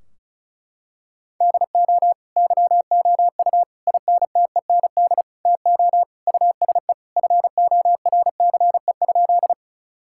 Morse code, heard as a keyed tone, about 34 wpm, 700 Hz. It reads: DO YOU INTEND TO USE FORCE?
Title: Sentences from Top 1600 Words 35wpm